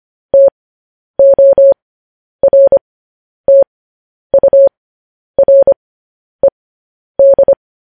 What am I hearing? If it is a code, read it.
Morse code, heard as a keyed tone, about 25 wpm, spaced as for 10 wpm, 550 Hz. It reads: TORTURED
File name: podcast / News headlines-1x-25x10